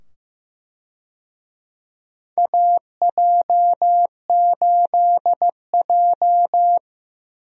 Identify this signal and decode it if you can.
Morse code, heard as a keyed tone, about 15 wpm, 700 Hz. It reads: AJ8J
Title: USA Call Signs - 4 Characters - 15wpm